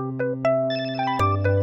There is no code.